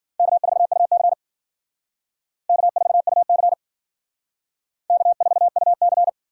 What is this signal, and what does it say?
Morse code, heard as a keyed tone, about 47 wpm, 700 Hz. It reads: K4UC K4UC K4UC
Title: Speed Racing - USA 1x2 Call Signs 40wpm